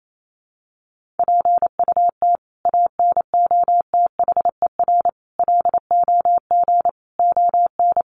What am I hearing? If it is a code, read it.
Morse code, heard as a keyed tone, about 28 wpm, 700 Hz. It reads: PUT ANOTHER LOG ON
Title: Sentences from Top 1400 Words 28wpm